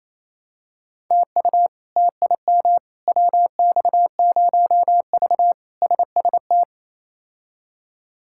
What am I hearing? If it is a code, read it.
Morse code, heard as a keyed tone, about 28 wpm, 700 Hz. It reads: TU TIM WX0V SST
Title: K1USN SST QSO Practice - 28wpm